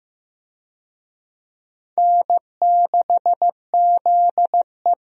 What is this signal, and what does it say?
Morse code, heard as a keyed tone, about 15 wpm, 700 Hz. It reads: N6ZE
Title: USA Call Signs - 4 Characters - Repeated 3x - 15wpm